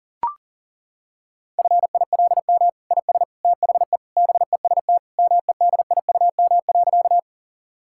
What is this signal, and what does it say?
Morse code, heard as a keyed tone, about 40 wpm, 700 Hz. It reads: FILM IS THE BEST MEDIUM.
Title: Sentences from Top 1000 Words - Review Part 6 - 40wpm